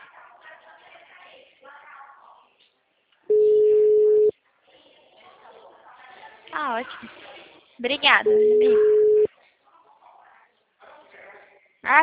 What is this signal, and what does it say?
Ringback tone (Europe, 425 Hz), with speech.